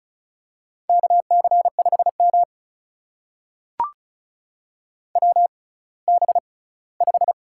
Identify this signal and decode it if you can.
Morse code, heard as a keyed tone, about 36 wpm, spaced as for 20 wpm, 700 Hz. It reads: KC5M WB5
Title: USA Call Signs - 6x Character Spacing - 35wpm